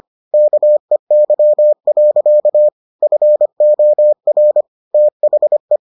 Morse code, heard as a keyed tone, about 25 wpm, 600 Hz. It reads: KEY. FOR THE